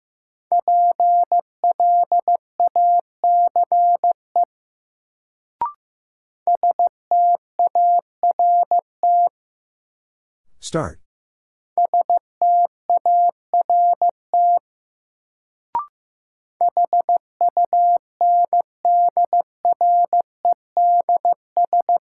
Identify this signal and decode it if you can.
Morse code, heard as a keyed tone, about 15 wpm, 700 Hz. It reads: PLACE START START HUNDREDS
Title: Sets of 4 Words - New Word Drill - 15wpm